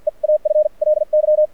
Morse code, heard as a keyed tone, about 57 wpm, spaced as for 42 wpm, 590 Hz. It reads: EAFLY
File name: archived / CRSnet